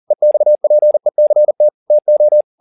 Morse code, heard as a keyed tone, about 40 wpm, 600 Hz. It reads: EXPECT TO
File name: the old man and the sea 2 40wpm